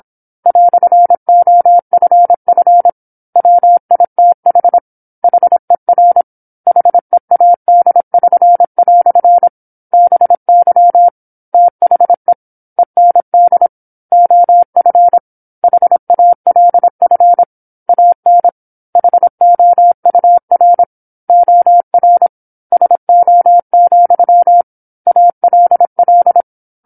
Morse code, heard as a keyed tone, about 27 wpm, 700 Hz. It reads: "OFF WITH HER HEAD<SN>" BY THE END OF HALF AN HOUR OR SO, ALL
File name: 08-Chapter-08 26 WPM